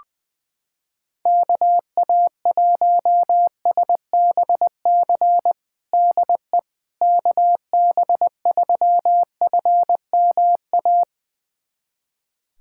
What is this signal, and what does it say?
Morse code, heard as a keyed tone, about 20 wpm, 700 Hz. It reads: KA1SBC DE KB3FMA
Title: USA Callsign DE Callsign - 6 Characters - 20wpm